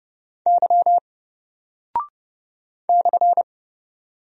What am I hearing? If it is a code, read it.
Morse code, heard as a keyed tone, about 29 wpm, 700 Hz. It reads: Y /
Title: Single Character (Letter:Number:Punctuation:Prosign) 30wpm